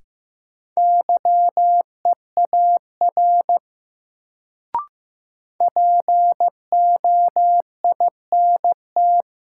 Morse code, heard as a keyed tone, about 15 wpm, 700 Hz. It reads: YEAR POINT
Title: Sentences from Top 200 Words - New Word Drill - 15wpm